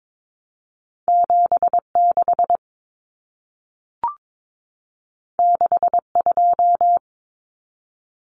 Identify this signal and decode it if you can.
Morse code, heard as a keyed tone, about 22 wpm, 700 Hz. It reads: 76 62
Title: Common Cut Numbers - 2-Digit 22wpm